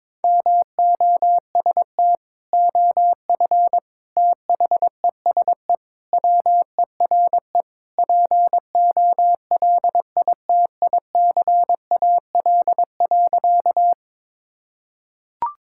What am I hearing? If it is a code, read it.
Morse code, heard as a keyed tone, about 22 wpm, 700 Hz. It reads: MOST OF THESE WERE POLITICAL.
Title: Sentences from Top 1000 Words - Review Part 5 - 22wpm